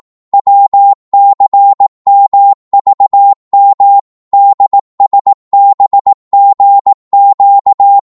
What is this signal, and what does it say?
Morse code, heard as a keyed tone, about 18 wpm, 800 Hz. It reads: WCMVM DSBGQ
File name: Morse / ltr-18wpm